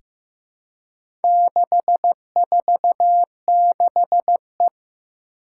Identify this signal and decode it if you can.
Morse code, heard as a keyed tone, about 15 wpm, 700 Hz. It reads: 646E
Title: Four Challenging Letter-Number Combinations 15wpm